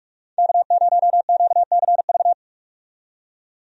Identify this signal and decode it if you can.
Morse code, heard as a keyed tone, about 47 wpm, 700 Hz. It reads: K0QCV
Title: USA Call Signs - 5 Characters - 45wpm